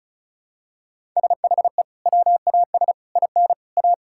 Morse code, heard as a keyed tone, about 36 wpm, 700 Hz. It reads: SHE WAS IN A
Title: Sentences from Top 1000 Words - Review Part 5 - 35wpm